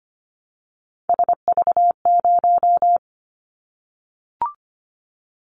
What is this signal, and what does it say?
Morse code, heard as a keyed tone, about 25 wpm, 700 Hz. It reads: SV0
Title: Callsign Prefixes for Top 50 DXCC - 25wpm